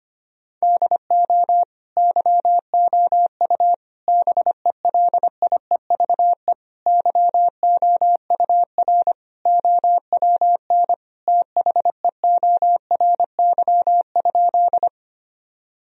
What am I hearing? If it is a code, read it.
Morse code, heard as a keyed tone, about 25 wpm, 700 Hz. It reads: DO YOU BELIEVE YOUR OWN THEORY?
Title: Sentences from Top 1000 Words - Review Part 3 - 25wpm